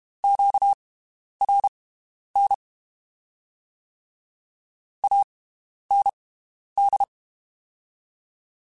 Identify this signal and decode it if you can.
Morse code, heard as a keyed tone, about 32 wpm, spaced as for 11 wpm, 800 Hz. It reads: QRN AND
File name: (spaced) / QSOs Spaced-mp3s / WD0BGP 1 @30 sp